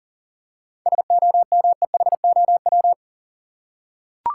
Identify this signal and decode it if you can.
Morse code, heard as a keyed tone, about 40 wpm, 700 Hz. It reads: SOMEHOW
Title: Seven Letter Words - 40wpm